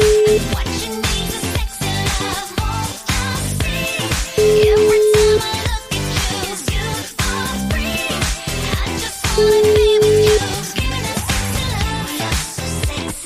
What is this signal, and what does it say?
Ringback tone (Europe, 425 Hz), with noise.